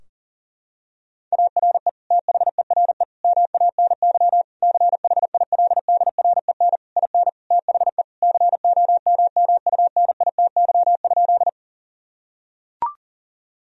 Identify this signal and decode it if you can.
Morse code, heard as a keyed tone, about 40 wpm, 700 Hz. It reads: ARE THERE MANY CHILDREN IN THE COMMUNITY?
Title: Sentences from Top 1000 Words - Review Part 4 - 40wpm